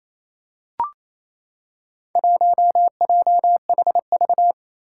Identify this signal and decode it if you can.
Morse code, heard as a keyed tone, about 27 wpm, 700 Hz. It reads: 1JHV
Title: Four Challenging Letter-Number Combinations 28wpm